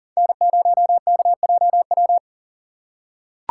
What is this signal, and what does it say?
Morse code, heard as a keyed tone, about 40 wpm, 700 Hz. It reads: N0KJW
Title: USA Call Signs - 5 Characters - 2x Character Spacing - 40wpm